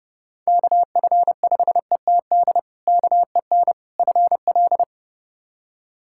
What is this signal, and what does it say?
Morse code, heard as a keyed tone, about 32 wpm, 700 Hz. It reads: KF5ETD KEN FL
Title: K1USN SST QSO Practice - 30wpm